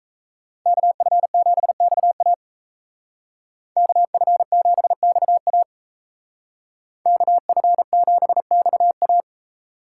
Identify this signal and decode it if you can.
Morse code, heard as a keyed tone, about 37 wpm, 700 Hz. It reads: KF7XA KF7XA KF7XA